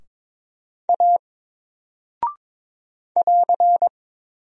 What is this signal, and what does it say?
Morse code, heard as a keyed tone, about 22 wpm, 700 Hz. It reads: A <AR>
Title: Single Character (Letter:Number:Punctuation:Prosign) 22wpm